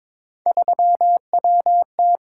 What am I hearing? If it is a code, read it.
Morse code, heard as a keyed tone, about 23 wpm, 700 Hz. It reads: 3WT
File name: practice / 22wpm / Three Letter-Number Combinations - 22wpm